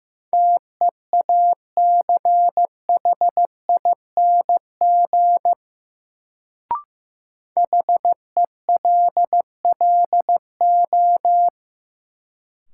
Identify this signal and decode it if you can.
Morse code, heard as a keyed tone, about 15 wpm, 700 Hz. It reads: TEACHING HELLO
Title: Sentences from Top 1100 Words - New Word Drill Alt - 15wpm